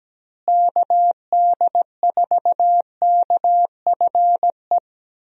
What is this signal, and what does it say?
Morse code, heard as a keyed tone, about 17 wpm, 700 Hz. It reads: KD4KFE
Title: USA Call Signs - 6 Characters - Repeated 3x - 17wpm